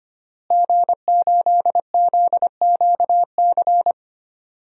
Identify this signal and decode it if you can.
Morse code, heard as a keyed tone, about 25 wpm, 700 Hz. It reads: G8ZQC
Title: DXCC-100 Call Signs - 5 Characters - Repeated 3x - 25wpm